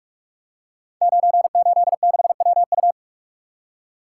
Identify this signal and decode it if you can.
Morse code, heard as a keyed tone, about 44 wpm, 700 Hz. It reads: 986WU
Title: Code Groups - 5 Letter:Numbers - 45wpm